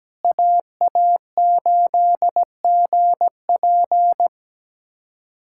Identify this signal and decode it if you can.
Morse code, heard as a keyed tone, about 17 wpm, 700 Hz. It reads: AA8GP
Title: USA Callsign DE Callsign - 5 Characters - 17wpm